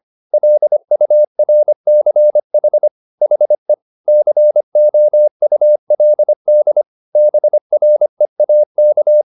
Morse code, heard as a keyed tone, about 25 wpm, 600 Hz. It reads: LURCH HE COULD BREAK